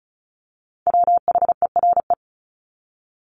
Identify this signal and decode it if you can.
Morse code, heard as a keyed tone, about 34 wpm, 700 Hz. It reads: WHERE
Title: Top 300 Words 35wpm